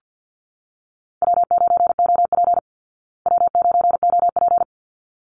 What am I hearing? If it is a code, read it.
Morse code, heard as a keyed tone, about 50 wpm, 700 Hz. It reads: W9OP W9OP
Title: Call Signs Repeated 3x - 50wpm